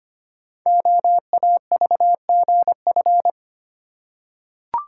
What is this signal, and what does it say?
Morse code, heard as a keyed tone, about 25 wpm, 700 Hz. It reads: OAVGF
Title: Code Groups - 5 Letters - 25wpm